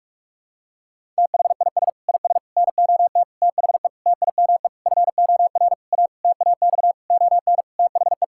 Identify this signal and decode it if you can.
Morse code, heard as a keyed tone, about 46 wpm, 700 Hz. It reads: THIS IS NOT THE TIME FOR A TAX ON THE
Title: Sentences from Top 1000 Words - Review Part 5 - 45wpm